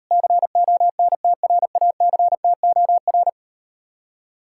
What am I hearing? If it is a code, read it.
Morse code, heard as a keyed tone, about 38 wpm, 700 Hz. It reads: CONTRACTOR